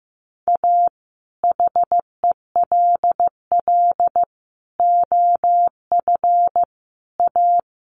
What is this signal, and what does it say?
Morse code, heard as a keyed tone, about 15 wpm, 700 Hz. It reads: A HELL OF A
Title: Sets of 5 Words Encore 15wpm